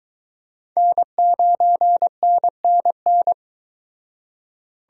Morse code, heard as a keyed tone, about 22 wpm, 700 Hz. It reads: N9NNN